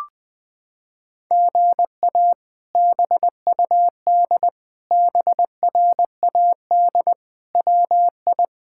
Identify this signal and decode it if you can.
Morse code, heard as a keyed tone, about 21 wpm, 700 Hz. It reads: GA BUD BRAD WI
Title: K1USN SST QSO Practice - 20wpm